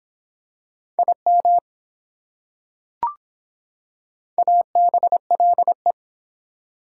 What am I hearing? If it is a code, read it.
Morse code, heard as a keyed tone, about 24 wpm, 700 Hz. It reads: IM ABLE